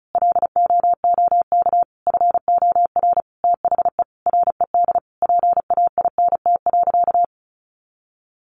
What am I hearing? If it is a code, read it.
Morse code, heard as a keyed tone, about 34 wpm, 700 Hz. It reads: LOOK FOR THE RED PAINT.
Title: A Sound Of Thunder 35wpm